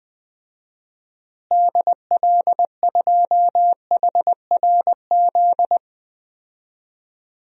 Morse code, heard as a keyed tone, about 21 wpm, 700 Hz. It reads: DL2HRZ
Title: DXCC-100 Call Signs - 6 Characters - Repeated 3x - 20wpm